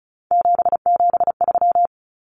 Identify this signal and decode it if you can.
Morse code, heard as a keyed tone, about 34 wpm, 700 Hz. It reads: 773
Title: Common Cut Numbers - 3-Digit 35wpm